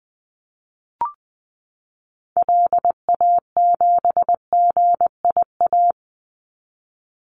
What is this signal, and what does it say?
Morse code, heard as a keyed tone, about 20 wpm, 700 Hz. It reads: LA7GIA
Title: Call Signs Repeated 2x - 20wpm